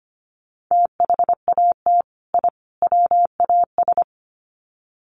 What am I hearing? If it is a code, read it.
Morse code, heard as a keyed tone, about 25 wpm, 700 Hz. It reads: THAT I WAS
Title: Sets of 3 Words 25wpm